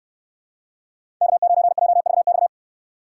Morse code, heard as a keyed tone, about 63 wpm, 700 Hz. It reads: K9JVX